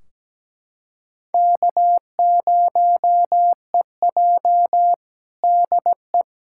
Morse code, heard as a keyed tone, about 17 wpm, 700 Hz. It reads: K0EJ DE
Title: Practice QSO - Exchange 4 - Ending - 17wpm